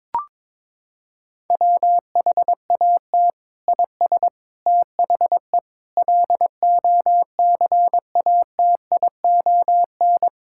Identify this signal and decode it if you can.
Morse code, heard as a keyed tone, about 21 wpm, 700 Hz. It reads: WHAT IS THE LOCATION
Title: Sentences from Top 1000 Words - Review Part 6 - 22wpm